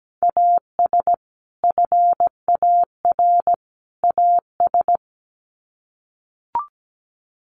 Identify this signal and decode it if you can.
Morse code, heard as a keyed tone, about 17 wpm, 700 Hz. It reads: AS FAR AS
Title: Sets of 3 Words 17wpm